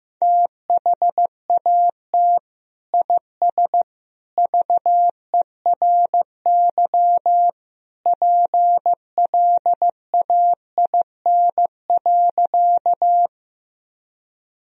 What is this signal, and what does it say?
Morse code, heard as a keyed tone, about 16 wpm, 700 Hz. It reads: THAT IS VERY PLAIN.
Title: Sentences from Top 1000 Words - Review Part 2 - 15wpm